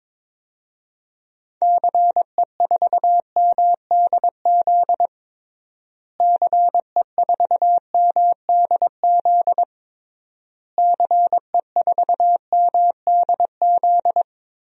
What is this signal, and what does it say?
Morse code, heard as a keyed tone, about 22 wpm, 700 Hz. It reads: CE4MDZ CE4MDZ CE4MDZ
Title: DXCC-100 Call Signs - 6 Characters - Repeated 3x - 22wpm